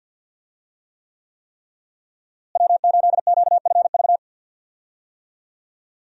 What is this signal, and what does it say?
Morse code, heard as a keyed tone, about 47 wpm, 700 Hz. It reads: W8QFV
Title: USA Call Signs - 5 Characters - Repeated 3x - 50wpm